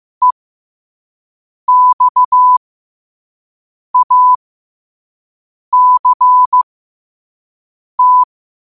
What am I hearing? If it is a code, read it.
Morse code, heard as a keyed tone, about 15 wpm, spaced as for 5 wpm, 1000 Hz. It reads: EXACT
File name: qotdfiles / 5wpm